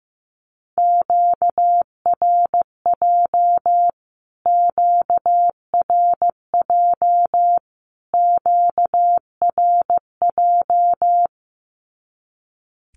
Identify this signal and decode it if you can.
Morse code, heard as a keyed tone, about 15 wpm, 700 Hz. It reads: QRJ QRJ QRJ
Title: Q-Codes Repeated 3x 15wpm